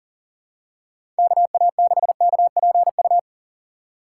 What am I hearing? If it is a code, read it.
Morse code, heard as a keyed tone, about 40 wpm, 700 Hz. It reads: KA6KPU
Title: USA Call Signs - 6 Characters - 4x Character Spacing - 40wpm